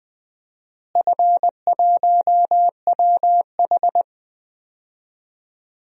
Morse code, heard as a keyed tone, about 20 wpm, 700 Hz. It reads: F1WH
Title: DXCC-100 Call Signs - 4 Characters - Repeated 3x - 20wpm